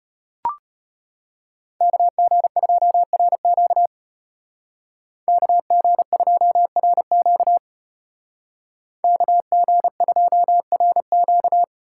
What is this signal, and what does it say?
Morse code, heard as a keyed tone, about 34 wpm, 700 Hz. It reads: KG2RQ KG2RQ KG2RQ